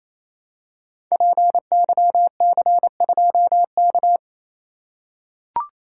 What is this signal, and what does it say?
Morse code, heard as a keyed tone, about 28 wpm, 700 Hz. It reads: PYC2K
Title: Code Groups - 5 Letter:Numbers - 28wpm